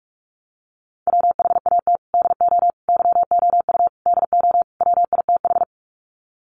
Morse code, heard as a keyed tone, about 44 wpm, 700 Hz. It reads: WHAT DO YOU DO WITH
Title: Sets of 5 Words Encore2 45wpm